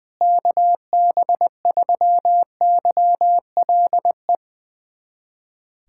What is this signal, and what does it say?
Morse code, heard as a keyed tone, about 21 wpm, 700 Hz. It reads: KB3YLE
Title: USA Callsign DE Callsign - 6 Characters - 20wpm